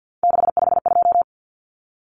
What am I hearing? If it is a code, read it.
Morse code, heard as a keyed tone, about 50 wpm, 700 Hz. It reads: 652
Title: Numbers - 3-Digit 50wpm